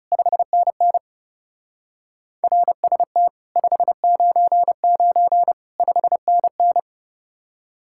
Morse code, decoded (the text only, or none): 5NN RST 599 5NN